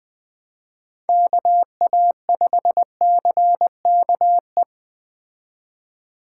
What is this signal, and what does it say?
Morse code, heard as a keyed tone, about 20 wpm, 700 Hz. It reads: KA5CKE